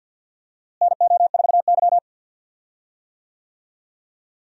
Morse code, heard as a keyed tone, about 48 wpm, 700 Hz. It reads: NO4Y
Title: USA Call Signs - 4 Characters - Repeated 2x - 50wpm